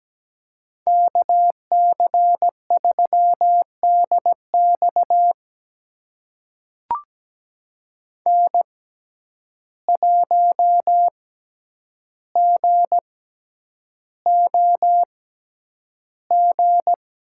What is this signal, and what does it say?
Morse code, heard as a keyed tone, about 17 wpm, spaced as for 9 wpm, 700 Hz. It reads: KC3DX N1GOG
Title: USA Call Signs - 5 Characters - 6x Character Spacing - 17wpm